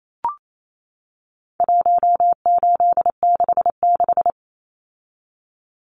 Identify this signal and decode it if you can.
Morse code, heard as a keyed tone, about 28 wpm, 700 Hz. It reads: 1866
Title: Numbers - 4-Digit 28wpm